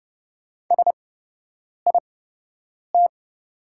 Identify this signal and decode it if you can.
Morse code, heard as a keyed tone, about 30 wpm, spaced as for 8 wpm, 700 Hz. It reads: SIT
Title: 8x Character Spacing - Top 500 Words - 30wpm